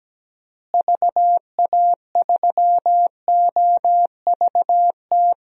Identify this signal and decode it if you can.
Morse code, heard as a keyed tone, about 17 wpm, 700 Hz. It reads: VA3OVT
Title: DXCC-100 Call Signs - 6 Characters - Repeated 3x - 17wpm